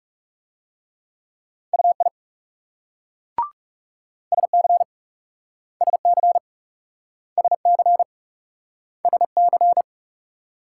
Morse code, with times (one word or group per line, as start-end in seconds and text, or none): 1.7-2.1 UI
4.3-4.8 SC
5.8-6.4 SC
7.4-8.0 SC
9.1-9.8 SC